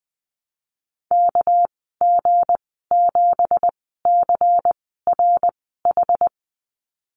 Morse code, heard as a keyed tone, about 20 wpm, spaced as for 15 wpm, 700 Hz. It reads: KG7CRH
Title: 207 Lesson 33g - Letter j - Call Signs 20wpm